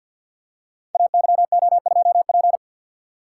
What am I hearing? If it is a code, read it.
Morse code, heard as a keyed tone, about 48 wpm, 700 Hz. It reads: AYO2P
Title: Code Groups - 5 Letter:Numbers - 50wpm